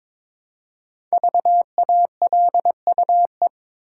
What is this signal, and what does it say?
Morse code, heard as a keyed tone, about 22 wpm, 690 Hz. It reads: VALUE
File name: 5 Letter Words - First 3 Letters Given - 22wpm